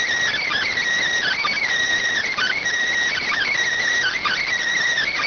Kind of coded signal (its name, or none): Emergency Alert System (SAME) tones